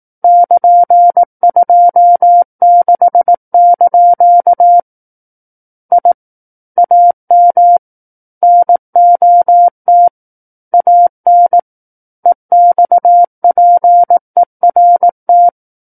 Morse code, heard as a keyed tone, about 18 wpm, 700 Hz. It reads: <KN>26) I AM NOT AN EXPERT
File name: Book One / 09-Chapter 9 - 18 WPM